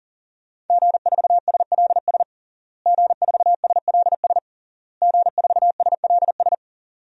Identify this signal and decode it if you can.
Morse code, heard as a keyed tone, about 40 wpm, 700 Hz. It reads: G4SLS G4SLS G4SLS
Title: DXCC-100 Call Signs - 5 Characters - Repeated 3x - 40wpm